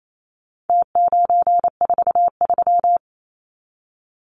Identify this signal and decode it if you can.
Morse code, heard as a keyed tone, about 28 wpm, 700 Hz. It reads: T943
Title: Numbers - Contest Style - 4-Digit 28wpm